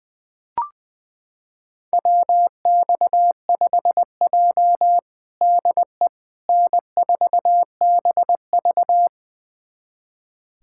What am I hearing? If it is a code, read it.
Morse code, heard as a keyed tone, about 20 wpm, 700 Hz. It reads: WX5J DE N4BV
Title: USA Callsign DE Callsign - 4 Characters - 20wpm